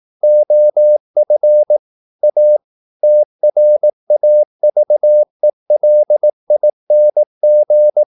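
Morse code, heard as a keyed tone, about 18 wpm, 600 Hz. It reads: OF A TRAVELING
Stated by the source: Morse practice at 18 wpm, English text, from CW med Gnister.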